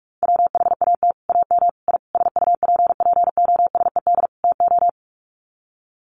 Morse code, heard as a keyed tone, about 44 wpm, 700 Hz. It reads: WHAT AM I SUPPOSED TO
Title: Sets of 5 Words Encore2 45wpm